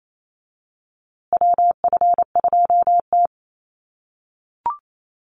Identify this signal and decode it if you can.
Morse code, heard as a keyed tone, about 28 wpm, 700 Hz. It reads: WF2T
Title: USA Call Signs Repeated 2x 28wpm